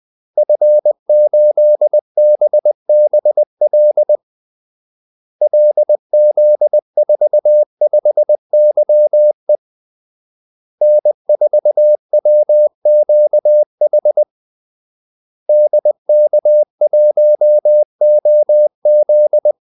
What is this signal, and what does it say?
Morse code, heard as a keyed tone, about 20 wpm, 600 Hz. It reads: F8BBL LZ45YE N4WQH DK1OZ
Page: Kaldesignaler 20 wpm